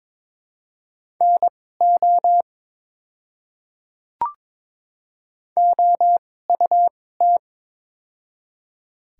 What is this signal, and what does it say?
Morse code, heard as a keyed tone, about 22 wpm, spaced as for 16 wpm, 700 Hz. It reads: NO OUT
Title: 2x Character Spacing - Top 100 Words - 22wpm